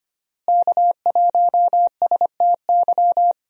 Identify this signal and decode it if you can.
Morse code, heard as a keyed tone, about 25 wpm, 700 Hz. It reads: K1STY
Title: USA Call Signs - 5 Characters - Repeated 3x - 25wpm